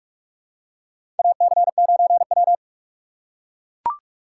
Morse code, heard as a keyed tone, about 44 wpm, 700 Hz. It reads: AC9W
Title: USA Call Signs - 6x Character Spacing - 45wpm